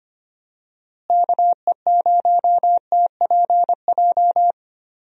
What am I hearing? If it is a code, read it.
Morse code, heard as a keyed tone, about 25 wpm, 700 Hz. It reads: KE0TPJ
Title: USA Call Signs - 6 Characters - Repeated 2x - 25wpm